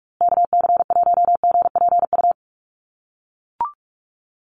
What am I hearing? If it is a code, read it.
Morse code, heard as a keyed tone, about 44 wpm, 700 Hz. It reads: KC1GPU
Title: USA Call Signs Repeated 2x 45wpm